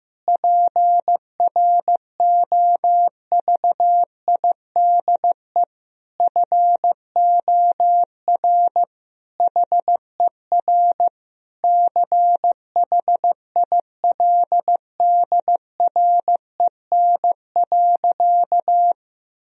Morse code, heard as a keyed tone, about 16 wpm, 700 Hz. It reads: PROVIDE FOR HER CHILDREN.
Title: Sentences from Top 1000 Words - Review Part 3 - 15wpm